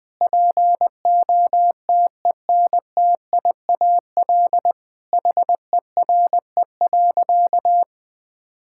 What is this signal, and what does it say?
Morse code, heard as a keyed tone, about 20 wpm, 700 Hz. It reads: POTENTIAL HERE.
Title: Sentences with 6-Word Phrases - Spoken Beginning - 20wpm